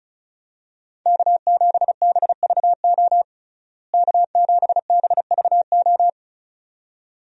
Morse code, heard as a keyed tone, about 34 wpm, 700 Hz. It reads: K7BVO K7BVO
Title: USA Call Signs - 5 Characters - Repeated 2x - 35wpm